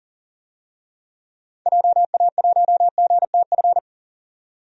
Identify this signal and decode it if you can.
Morse code, heard as a keyed tone, about 40 wpm, 700 Hz. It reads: JA1GTF
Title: DXCC-100 Call Signs - 6 Characters - 40wpm